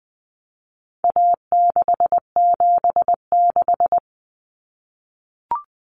Morse code, heard as a keyed tone, about 20 wpm, 700 Hz. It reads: A676
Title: Common Cut Numbers - 4-Digit 20wpm